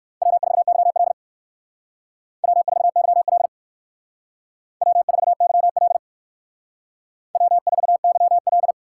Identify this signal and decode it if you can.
Morse code, heard as a keyed tone, about 58 wpm, 700 Hz. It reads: W4YL W4YL W4YL W4YL